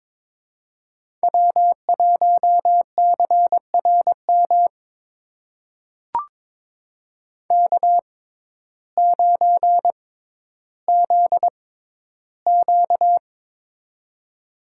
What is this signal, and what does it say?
Morse code, heard as a keyed tone, about 22 wpm, spaced as for 12 wpm, 700 Hz. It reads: W1CRM K9ZQ
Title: USA Call Signs - 6x Character Spacing - 22wpm